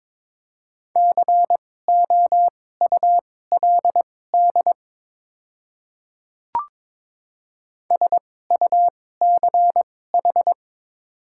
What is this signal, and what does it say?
Morse code, heard as a keyed tone, about 21 wpm, spaced as for 16 wpm, 700 Hz. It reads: COULD SUCH
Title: 2x Character Spacing - Top 200 Words - 22wpm